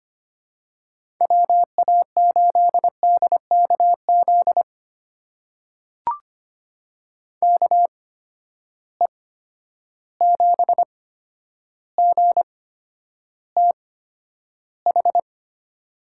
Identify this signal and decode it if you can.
Morse code, heard as a keyed tone, about 25 wpm, spaced as for 11 wpm, 700 Hz. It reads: WA8DKZ KE7GTH
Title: USA Call Signs - 6 Characters - 8x Character Spacing - 25wpm